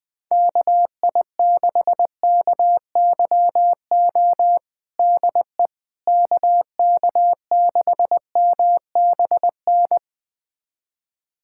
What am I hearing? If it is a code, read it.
Morse code, heard as a keyed tone, about 20 wpm, 700 Hz. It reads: KI6KYO DE KK6MBN
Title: USA Callsign DE Callsign - 6 Characters - 20wpm